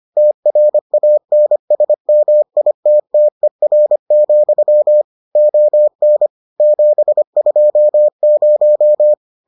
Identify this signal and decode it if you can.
Morse code, heard as a keyed tone, about 25 wpm, 600 Hz. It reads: TRANSMITTER, ON 720